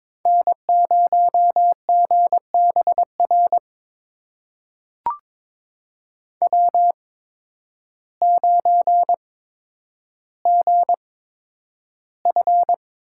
Morse code, heard as a keyed tone, about 22 wpm, spaced as for 10 wpm, 700 Hz. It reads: N0GBR W9GF
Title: USA Call Signs - 8x Character Spacing - 22wpm